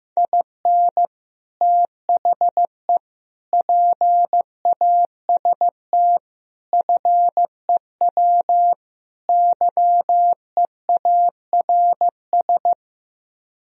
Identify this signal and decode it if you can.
Morse code, heard as a keyed tone, about 15 wpm, 700 Hz. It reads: IN THE PAST FEW YEARS
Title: Speed-Racing - Sets of 5 Words 15wpm